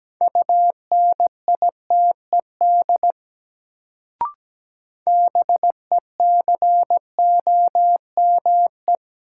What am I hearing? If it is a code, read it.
Morse code, heard as a keyed tone, about 17 wpm, 700 Hz. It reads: UNITED BECOME
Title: Sentences from Top 600 Words - New Word Drill - 17wpm